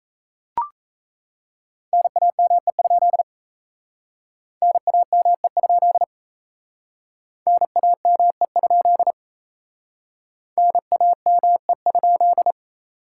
Morse code, heard as a keyed tone, about 36 wpm, 700 Hz. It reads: NAME? NAME? NAME? NAME?